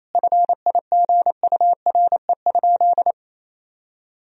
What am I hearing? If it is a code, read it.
Morse code, heard as a keyed tone, about 28 wpm, 700 Hz. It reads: FIGURE?
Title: Sentences from Top 1000 Words - Review Part 2 - 28wpm